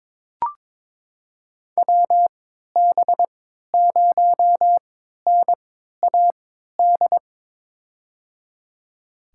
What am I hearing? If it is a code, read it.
Morse code, heard as a keyed tone, about 22 wpm, spaced as for 12 wpm, 700 Hz. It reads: WB0NAD
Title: USA Call Signs - 6 Characters - 3x Character Spacing - 22wpm